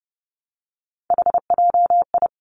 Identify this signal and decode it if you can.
Morse code, heard as a keyed tone, about 30 wpm, 700 Hz. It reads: HJI
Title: Three Challenging Letter-Number Combinations 30wpm